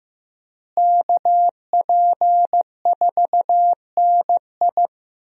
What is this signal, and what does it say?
Morse code, heard as a keyed tone, about 15 wpm, 700 Hz. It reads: KP4NI
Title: USA Call Signs - 5 Characters - Repeated 3x - 15wpm